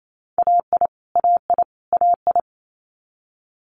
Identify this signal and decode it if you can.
Morse code, heard as a keyed tone, about 28 wpm, 700 Hz. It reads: AI AI AI
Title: Multi-letter Phonemes Repeated 3x 28wpm